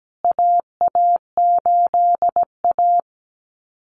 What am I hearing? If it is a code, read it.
Morse code, heard as a keyed tone, about 17 wpm, 700 Hz. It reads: AA8A
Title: CWT Exchange 17wpm